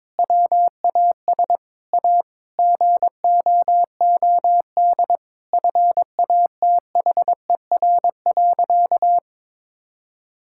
Morse code, heard as a keyed tone, about 21 wpm, 700 Hz. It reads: WAS A GOOD FATHER.
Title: Sentences from Top 1000 Words - Review Part 5 - 22wpm